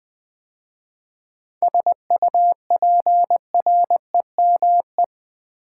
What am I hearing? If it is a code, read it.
Morse code, heard as a keyed tone, about 20 wpm, 700 Hz. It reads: SUPREME
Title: 7 Letter Words - First 3 Letters Given - 20wpm